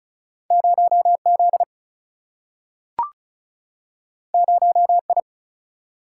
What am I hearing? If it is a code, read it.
Morse code, heard as a keyed tone, about 36 wpm, 700 Hz. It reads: 0 Z 0 I
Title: Single-Letter Single-Number 35wpm